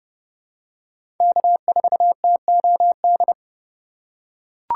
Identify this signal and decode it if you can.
Morse code, heard as a keyed tone, about 31 wpm, 700 Hz. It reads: K4TOD
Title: USA Call Signs - 5 Characters - 2x Character Spacing - 30wpm